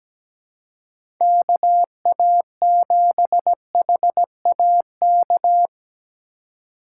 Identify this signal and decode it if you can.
Morse code, heard as a keyed tone, about 17 wpm, 700 Hz. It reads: KA7HAK
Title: USA Call Signs - 2x Character Spacing - 17wpm